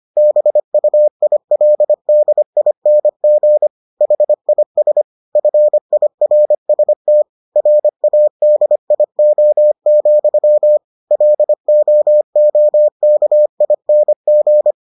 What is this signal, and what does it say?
Morse code, heard as a keyed tone, about 25 wpm, 600 Hz. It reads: BUILDING HIS FIRST RADIO, LOOKING